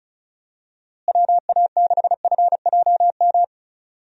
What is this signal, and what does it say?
Morse code, heard as a keyed tone, about 36 wpm, 700 Hz. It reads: WA6FJM
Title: USA Call Signs - 6 Characters - Repeated 2x - 35wpm